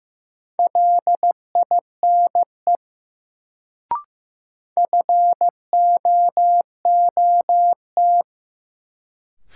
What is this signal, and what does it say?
Morse code, heard as a keyed tone, about 15 wpm, 700 Hz. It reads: LINE FOOT
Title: Sets of 5 Words - New Word Drill - 15wpm